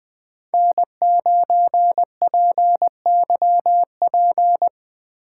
Morse code, heard as a keyed tone, about 20 wpm, 700 Hz. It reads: N9PYP
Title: Speed Racing - USA 1x3 Call Signs 17wpm